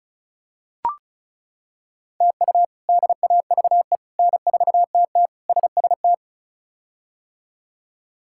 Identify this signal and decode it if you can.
Morse code, heard as a keyed tone, about 36 wpm, 700 Hz. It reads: TU DAVE N4TT SST
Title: K1USN SST QSO Practice - 35wpm